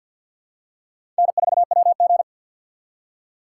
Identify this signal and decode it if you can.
Morse code, heard as a keyed tone, about 48 wpm, 690 Hz. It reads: N4WG
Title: USA Call Signs - 4 Characters - 6x Character Spacing - 50wpm